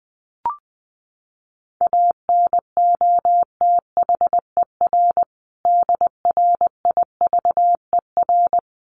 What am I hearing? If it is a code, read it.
Morse code, heard as a keyed tone, about 20 wpm, 700 Hz. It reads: ANOTHER DRIVER
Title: Sentences from Top 1400 Words 20wpm